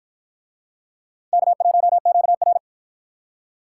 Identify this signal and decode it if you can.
Morse code, heard as a keyed tone, about 53 wpm, 700 Hz. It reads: K1QR